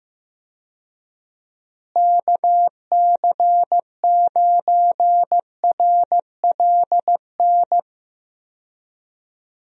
Morse code, heard as a keyed tone, about 15 wpm, 700 Hz. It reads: KC9RLN
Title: USA Call Signs - 6 Characters - Repeated 2x - 15wpm